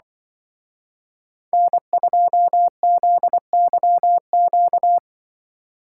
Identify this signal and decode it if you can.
Morse code, heard as a keyed tone, about 24 wpm, 700 Hz. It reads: N2ZYQ
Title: Speed Racing - USA 1x3 Call Signs 20wpm